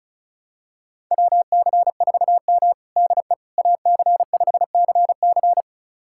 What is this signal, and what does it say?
Morse code, heard as a keyed tone, about 36 wpm, 700 Hz. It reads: WC4M DE AC5CC
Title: USA Callsign DE Callsign 35wpm